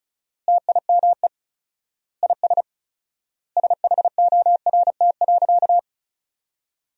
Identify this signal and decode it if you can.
Morse code, heard as a keyed tone, about 36 wpm, 700 Hz. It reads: TIME IS SHORT.
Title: 4x Word-Spacing - Sentences from Top 500 Words - 35wpm